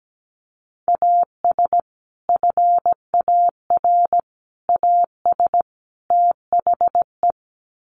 Morse code, heard as a keyed tone, about 17 wpm, 700 Hz. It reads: AS FAR AS THE
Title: Sets of 4 Words 17wpm